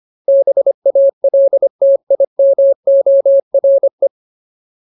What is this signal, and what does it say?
Morse code, heard as a keyed tone, about 25 wpm, 550 Hz. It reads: BALTIMORE
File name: Ultimate Word List / 25WPM Ultimate